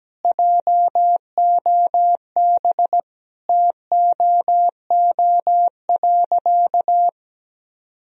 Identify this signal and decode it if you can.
Morse code, heard as a keyed tone, about 17 wpm, 700 Hz. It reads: JOB TOO.
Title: Sentences from Top 1000 Words - Review Part 3 - 17wpm